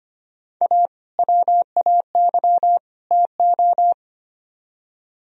Morse code, heard as a keyed tone, about 32 wpm, 700 Hz. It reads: A WAY TO